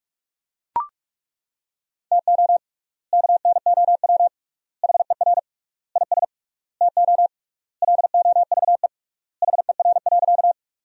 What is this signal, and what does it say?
Morse code, heard as a keyed tone, about 41 wpm, 700 Hz. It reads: TO KNOW HER IS TO LOVE HER.
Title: 3x Word-Spacing - Sentences from Top 400 Words - 45wpm